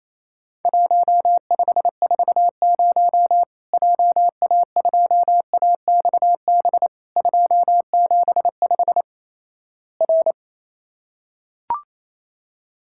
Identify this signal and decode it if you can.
Morse code, heard as a keyed tone, about 27 wpm, 700 Hz. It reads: 1540 JA2AXB 275 R
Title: Worked All Europe DX Contest - QTC Exchange - 28wpm